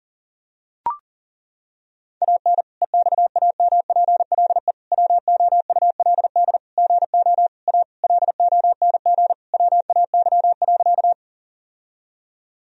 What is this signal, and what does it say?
Morse code, heard as a keyed tone, about 37 wpm, 700 Hz. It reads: AN EXAMPLE WOULD GO A LONG WAY.
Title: Sentences from Top 1000 Words - Review Part 1 - 40wpm